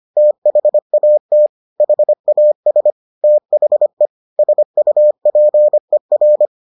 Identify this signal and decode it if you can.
Morse code, heard as a keyed tone, about 25 wpm, 600 Hz. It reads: THAT HAS THE SUPER